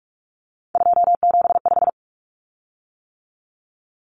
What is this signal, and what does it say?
Morse code, heard as a keyed tone, about 44 wpm, 700 Hz. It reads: 275
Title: Common Cut Numbers - 3-Digit 45wpm